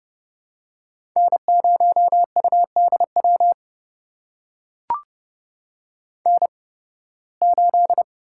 Morse code, heard as a keyed tone, about 30 wpm, spaced as for 19 wpm, 700 Hz. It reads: N0UDW N8
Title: USA Call Signs - 5 Characters - 8x Character Spacing - 30wpm